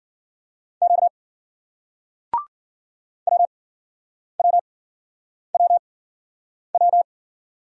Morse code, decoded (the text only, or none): Q W W W W